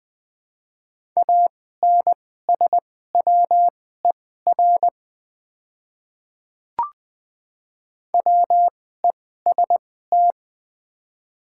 Morse code, heard as a keyed tone, about 21 wpm, spaced as for 15 wpm, 700 Hz. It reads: ANSWER WEST
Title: 2x Character Spacing - Top 500 Words - 20wpm